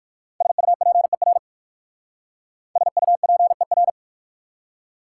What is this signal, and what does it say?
Morse code, heard as a keyed tone, about 58 wpm, 700 Hz. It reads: SUPER SUPER